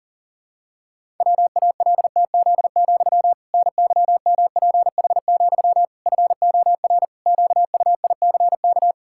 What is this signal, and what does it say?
Morse code, heard as a keyed tone, about 40 wpm, 700 Hz. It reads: WALTZ, NYMPH, FOR QUICK
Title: Pangram Challenge v2 - 40wpm